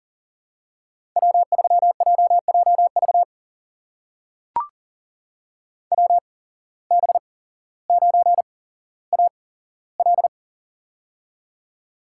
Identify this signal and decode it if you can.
Morse code, heard as a keyed tone, about 40 wpm, spaced as for 17 wpm, 700 Hz. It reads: W3JJV WB9AL
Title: USA Call Signs - 5 Characters - 8x Character Spacing - 40wpm